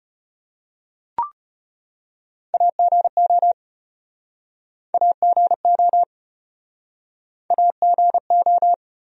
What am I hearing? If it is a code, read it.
Morse code, heard as a keyed tone, about 34 wpm, 700 Hz. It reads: AGO AGO AGO